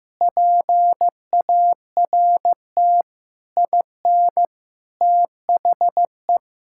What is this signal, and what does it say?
Morse code, heard as a keyed tone, about 16 wpm, 700 Hz. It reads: PART IN THE
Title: Sentences from Top 1000 Words - Review Part 6 - 15wpm